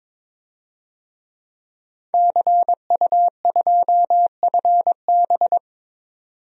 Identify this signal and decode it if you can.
Morse code, heard as a keyed tone, about 22 wpm, 700 Hz. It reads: CU2FB
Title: DXCC-100 Call Signs - 5 Characters - 22wpm